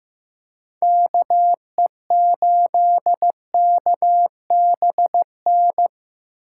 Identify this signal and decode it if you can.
Morse code, heard as a keyed tone, about 15 wpm, 700 Hz. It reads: KE8KBN
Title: USA Call Signs - 6 Characters - 6x Character Spacing - 15wpm